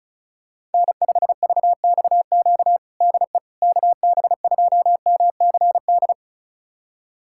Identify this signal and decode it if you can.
Morse code, heard as a keyed tone, about 36 wpm, 700 Hz. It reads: N5VXQ DE KB2MCD
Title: USA Callsign DE Callsign 35wpm